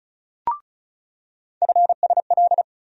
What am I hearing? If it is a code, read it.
Morse code, heard as a keyed tone, about 34 wpm, 700 Hz. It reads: FSL
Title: USA Call Sign Postfixes - 3 Letters 35wpm